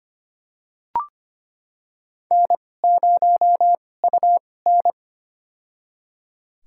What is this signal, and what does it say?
Morse code, heard as a keyed tone, about 25 wpm, spaced as for 18 wpm, 700 Hz. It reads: N0UN
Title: USA Call Signs - 4 Characters - 2x Character Spacing - 25wpm